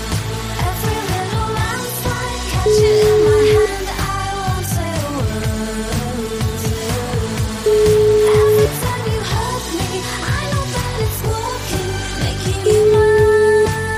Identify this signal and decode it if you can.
Ringback tone (Europe, 425 Hz), with noise.